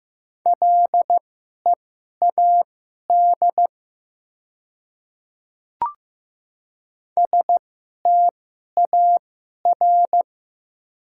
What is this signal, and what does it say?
Morse code, heard as a keyed tone, about 15 wpm, spaced as for 11 wpm, 700 Hz. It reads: LEAD STAR
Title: 2x Character Spacing - Top 500 Words - 15wpm